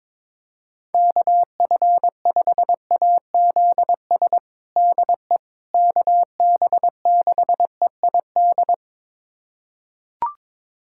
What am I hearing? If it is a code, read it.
Morse code, heard as a keyed tone, about 22 wpm, 700 Hz. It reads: KF5AZS DE KB6EID
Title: USA Callsign DE Callsign - 6 Characters - 22wpm